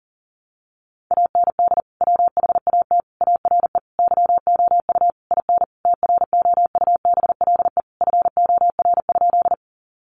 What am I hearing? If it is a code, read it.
Morse code, heard as a keyed tone, about 37 wpm, 700 Hz. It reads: AND WHAT ARE YOU IN TROUBLE FOR?
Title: Sentences from Top 1200 Words 40wpm